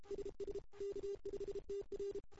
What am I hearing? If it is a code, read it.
Morse code, heard as a keyed tone, about 31 wpm, 410 Hz. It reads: SS K5TR
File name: CW / 2008-ss-cw-k5tr-op-n5rz -02:00